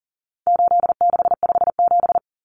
Morse code, heard as a keyed tone, about 40 wpm, 700 Hz. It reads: 8657
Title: Common Cut Numbers - 4-Digit 40wpm